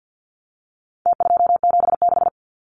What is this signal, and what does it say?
Morse code, heard as a keyed tone, about 50 wpm, 700 Hz. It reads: T276
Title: Numbers - Contest Style - 4-Digit 50wpm